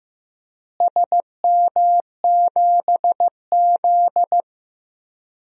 Morse code, heard as a keyed tone, about 15 wpm, 700 Hz. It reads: SM7Z
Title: DXCC-100 Call Signs - 4 Characters - Repeated 3x - 15wpm